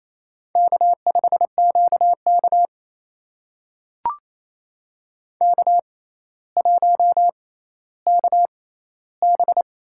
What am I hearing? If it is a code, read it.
Morse code, heard as a keyed tone, about 27 wpm, spaced as for 14 wpm, 700 Hz. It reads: K5QK K1KB
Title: USA Call Signs - 4 Characters - 6x Character Spacing - 28wpm